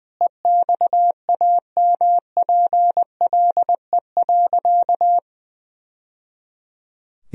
Morse code, heard as a keyed tone, about 21 wpm, 700 Hz. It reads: EXAMPLE.
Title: Sentences from Top 1000 Words - Review Part 1 - 20wpm